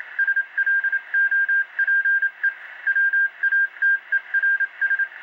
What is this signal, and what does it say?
Morse code, heard as a keyed tone, about 27 wpm, 1600 Hz. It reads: SLOPE WATERS